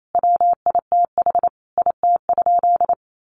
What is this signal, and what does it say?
Morse code, heard as a keyed tone, about 28 wpm, 700 Hz. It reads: WITH IT?
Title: Sentences from Top 600 Words 28wpm